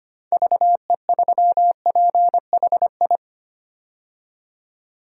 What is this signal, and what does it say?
Morse code, heard as a keyed tone, about 25 wpm, 690 Hz. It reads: VE3PHI
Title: DXCC-100 Call Signs - 6 Characters - 25wpm